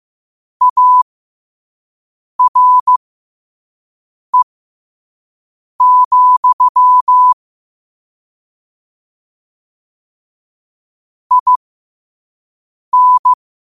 Morse code, heard as a keyed tone, about 15 wpm, spaced as for 5 wpm, 1000 Hz. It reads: ARE, IN
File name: qotdfiles / 5wpm